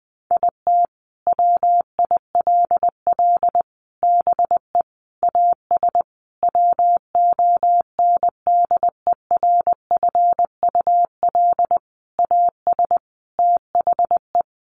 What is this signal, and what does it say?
Morse code, heard as a keyed tone, about 20 wpm, 700 Hz. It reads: IT WILL BE AS WONDERFUL AS THE
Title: Sentences from Top 1400 Words 20wpm